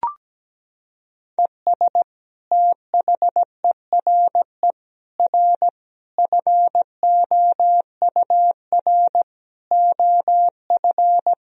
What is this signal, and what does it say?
Morse code, heard as a keyed tone, about 17 wpm, 700 Hz. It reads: ES THERE R FOUR OF
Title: Sentences from Top 300 Words and Abbreviations 17wpm